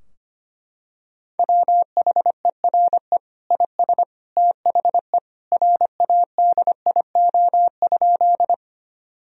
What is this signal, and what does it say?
Morse code, heard as a keyed tone, about 25 wpm, 700 Hz. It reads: WHERE IS THE RADIO?
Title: Sentences from Top 1000 Words - Review Part 3 - 25wpm